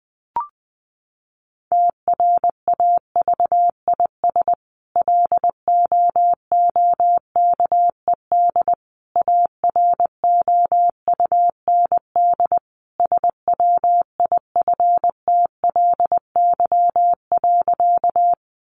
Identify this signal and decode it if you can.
Morse code, heard as a keyed tone, about 20 wpm, 700 Hz. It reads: TRAVIS LOOKED AROUND SWIFTLY.
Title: A Sound Of Thunder 20wpm